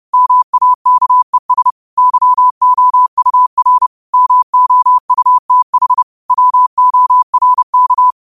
Morse code, heard as a keyed tone, about 30 wpm, 1000 Hz. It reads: MAKES YOUR MOUTH WORK
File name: qotdfiles / 30wpm